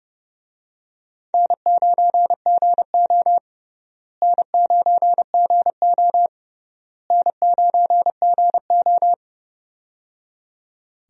Morse code, heard as a keyed tone, about 30 wpm, 700 Hz. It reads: N9GO N9GO N9GO
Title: USA Call Signs - 4 Characters - Repeated 3x - 30wpm